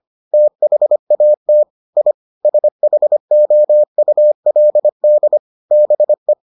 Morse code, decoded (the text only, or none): THAT I SHOULD BE